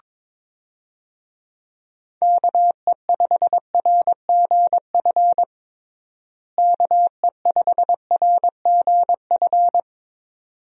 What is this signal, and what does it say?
Morse code, heard as a keyed tone, about 21 wpm, 700 Hz. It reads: KE5RGF KE5RGF
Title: USA Call Signs - 6 Characters - Repeated 2x - 22wpm